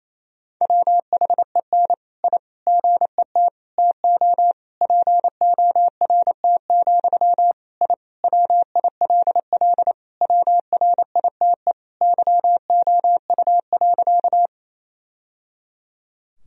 Morse code, heard as a keyed tone, about 28 wpm, 700 Hz. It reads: WHEN I GET TO PORT, I WILL WRITE YOU.
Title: Sentences from Top 1000 Words - Review Part 1 - 28wpm